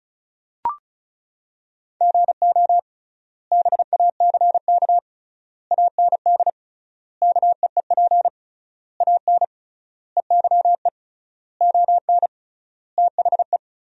Morse code, heard as a keyed tone, about 36 wpm, 700 Hz. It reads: GO BACK AND KEEP AN EYE ON THE
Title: 3x Word-Spacing - Sentences from Top 300 Words - 35wpm